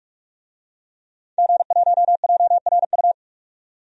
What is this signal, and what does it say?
Morse code, heard as a keyed tone, about 42 wpm, 700 Hz. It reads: G1JRU
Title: DXCC-100 Call Signs - 5 Characters - 45wpm